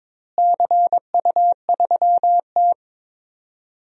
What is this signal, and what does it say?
Morse code, heard as a keyed tone, about 22 wpm, 700 Hz. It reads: CU3T
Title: DXCC-100 Call Signs - 4 Characters - Repeated 3x - 22wpm